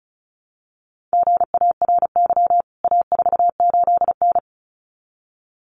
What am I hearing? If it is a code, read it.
Morse code, heard as a keyed tone, about 34 wpm, 700 Hz. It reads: GARY A48N
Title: CWT Exchange 35wpm